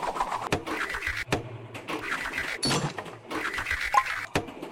Emergency Alert System (SAME) tones, heard as harsh buzzing bursts. It is heard from 0.8 to 3.8 s.